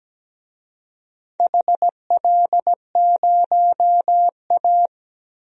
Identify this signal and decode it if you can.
Morse code, heard as a keyed tone, about 17 wpm, 700 Hz. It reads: HL0A
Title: DXCC-100 Call Signs - 4 Characters - Repeated 2x - 17wpm